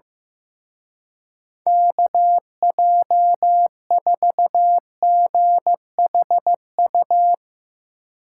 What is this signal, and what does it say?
Morse code, heard as a keyed tone, about 17 wpm, 700 Hz. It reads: KJ4GHU
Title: Speed Racing - USA 2x3 Call Signs 15wpm